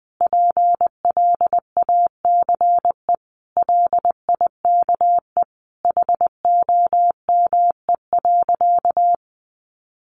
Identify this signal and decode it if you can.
Morse code, heard as a keyed tone, about 20 wpm, 700 Hz. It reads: PLACE LIKE HOME.
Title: Sentences from Top 200 Words 20wpm